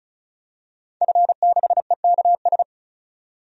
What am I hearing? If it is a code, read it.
Morse code, heard as a keyed tone, about 34 wpm, 700 Hz. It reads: F6EKS
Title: DXCC-100 Call Signs - 5 Characters - 35wpm